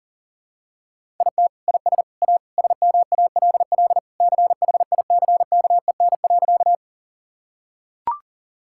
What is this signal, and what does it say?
Morse code, heard as a keyed tone, about 40 wpm, 700 Hz. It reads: IT IS A SMALL CHICKEN.
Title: Sentences from Top 1000 Words - Review Part 5 - 40wpm